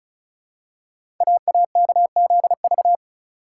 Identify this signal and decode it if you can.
Morse code, heard as a keyed tone, about 36 wpm, 700 Hz. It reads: AAKZV
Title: Code Groups - 5 Letter:Numbers - 35wpm